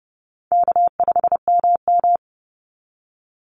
Morse code, heard as a keyed tone, about 30 wpm, 700 Hz. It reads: K5MM
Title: Easy Call Signs Repeated 3x - 30wpm